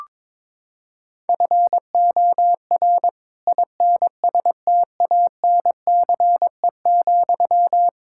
Morse code, heard as a keyed tone, about 22 wpm, 700 Hz. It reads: FOR INSTANCE,
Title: Sentences from Top 1000 Words - Review Part 5 - 22wpm